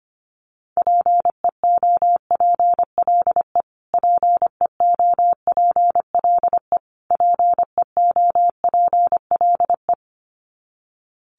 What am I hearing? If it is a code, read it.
Morse code, heard as a keyed tone, about 25 wpm, 700 Hz. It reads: PEOPLE PEOPLE PEOPLE
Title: Top 200 Words Repeated 25wpm